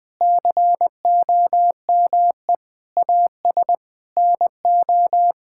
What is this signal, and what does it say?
Morse code, heard as a keyed tone, about 20 wpm, 700 Hz. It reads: COME AS NO
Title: Sentences with 8-Word Phrases - Spoken Beginning - 20wpm